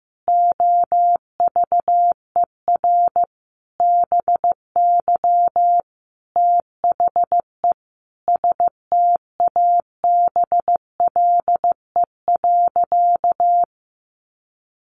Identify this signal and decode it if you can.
Morse code, heard as a keyed tone, about 15 wpm, 700 Hz. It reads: OVER BY THE STABLE.
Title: Sentences from Top 1500 Words 15wpm